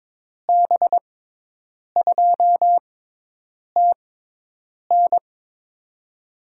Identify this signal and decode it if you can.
Morse code, heard as a keyed tone, about 22 wpm, spaced as for 8 wpm, 700 Hz. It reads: B2TN
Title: USA Call Signs - 6x Character Spacing - 22wpm